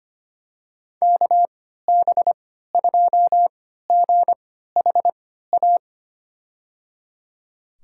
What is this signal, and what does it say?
Morse code, heard as a keyed tone, about 25 wpm, spaced as for 14 wpm, 700 Hz. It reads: KB2GHA
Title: USA Call Signs - 6 Characters - 3x Character Spacing - 25wpm